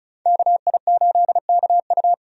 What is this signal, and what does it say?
Morse code, heard as a keyed tone, about 36 wpm, 700 Hz. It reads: KI8KU
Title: USA Call Signs - 5 Characters - 35wpm